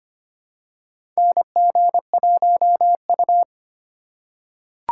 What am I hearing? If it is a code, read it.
Morse code, heard as a keyed tone, about 25 wpm, 700 Hz. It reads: NG1U
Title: USA Call Signs - 4 Characters - 3x Character Spacing - 25wpm